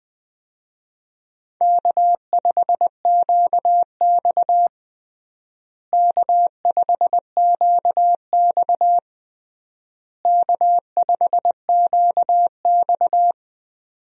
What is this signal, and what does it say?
Morse code, heard as a keyed tone, about 21 wpm, 700 Hz. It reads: K5QX K5QX K5QX
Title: USA Call Signs - 4 Characters - Repeated 3x - 20wpm